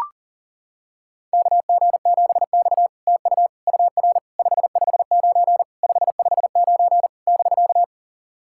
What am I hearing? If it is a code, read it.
Morse code, heard as a keyed tone, about 40 wpm, 700 Hz. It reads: KG7X TU UR 559 559 <BK>
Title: SOTA QSO Practice 40wpm